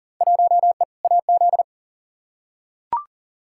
Morse code, heard as a keyed tone, about 37 wpm, 700 Hz. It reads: 1E AZ
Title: Field Day Exchanges - 40wpm